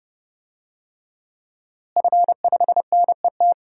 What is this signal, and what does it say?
Morse code, heard as a keyed tone, about 31 wpm, 700 Hz. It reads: F5NET
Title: DXCC-100 Call Signs - 5 Characters - Repeated 3x - 30wpm